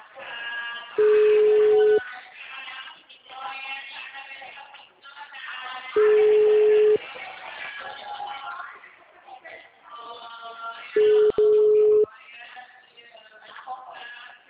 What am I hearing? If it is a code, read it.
Ringback tone (Europe, 425 Hz), with speech.
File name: out- -2041- - - .51346